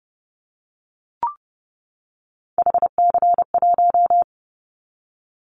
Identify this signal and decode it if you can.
Morse code, heard as a keyed tone, about 29 wpm, 700 Hz. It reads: HC1
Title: Callsign Prefixes for Top 100 DXCC - 30wpm